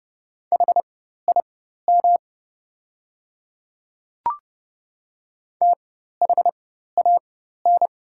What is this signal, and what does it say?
Morse code, heard as a keyed tone, about 30 wpm, spaced as for 14 wpm, 700 Hz. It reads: HIM THAN
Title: 4x Character Spacing - Top 100 Words - 30wpm